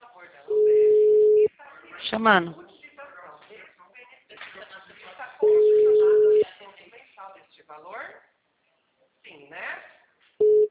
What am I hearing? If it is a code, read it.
Ringback tone (Europe, 425 Hz), with speech.